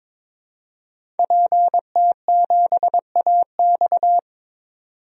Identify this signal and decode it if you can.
Morse code, heard as a keyed tone, about 22 wpm, 700 Hz. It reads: PT7AX
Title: DXCC-100 Call Signs - 5 Characters - 22wpm